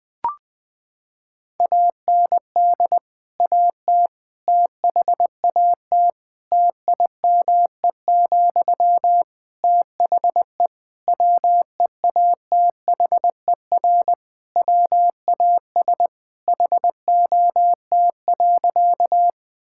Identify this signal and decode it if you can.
Morse code, heard as a keyed tone, about 21 wpm, 700 Hz. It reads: AND AT THAT TIME, THE WEATHER WAS HOT.
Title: Sentences from Top 1000 Words - Review Part 5 - 20wpm